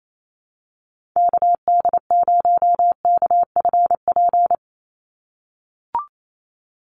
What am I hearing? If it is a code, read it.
Morse code, heard as a keyed tone, about 28 wpm, 700 Hz. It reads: KD0KFP
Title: USA Call Signs 28wpm